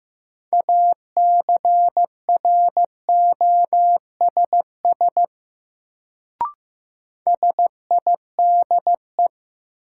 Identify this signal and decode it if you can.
Morse code, heard as a keyed tone, about 15 wpm, 700 Hz. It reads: ACROSS SIDE
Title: Sets of 2 Words - New Word Drill - 15wpm